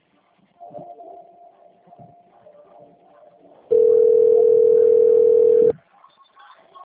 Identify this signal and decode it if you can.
Ringback tone (North America), 440 + 480 Hz.
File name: out- - - - - .77402